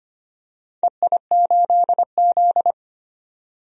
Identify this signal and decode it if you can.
Morse code, heard as a keyed tone, about 25 wpm, 700 Hz. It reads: EI8Z
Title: DXCC-100 Call Signs - 4 Characters - 25wpm